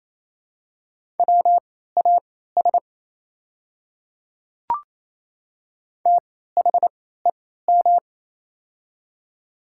Morse code, heard as a keyed tone, about 28 wpm, spaced as for 16 wpm, 700 Hz. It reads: WAS THEM
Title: 3x Character Spacing - Top 200 Words - 28wpm